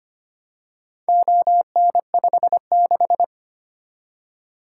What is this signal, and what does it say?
Morse code, heard as a keyed tone, about 25 wpm, 700 Hz. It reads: ON56
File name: practice / 25wpm / Four Letter-Number Combinations - 25wpm